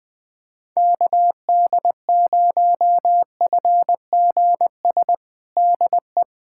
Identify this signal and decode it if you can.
Morse code, heard as a keyed tone, about 20 wpm, 700 Hz. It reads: KD0FGS DE
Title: USA Callsign DE Callsign - 6 Characters - 20wpm